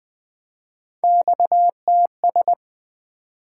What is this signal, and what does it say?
Morse code, heard as a keyed tone, about 20 wpm, 700 Hz. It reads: XTS
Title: Three Letter-Number Combinations - 20wpm